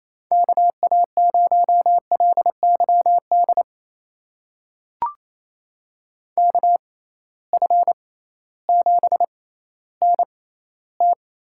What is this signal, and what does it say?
Morse code, heard as a keyed tone, about 28 wpm, spaced as for 15 wpm, 700 Hz. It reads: KA0LYD KF7NT
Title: USA Call Signs - 6 Characters - 6x Character Spacing - 28wpm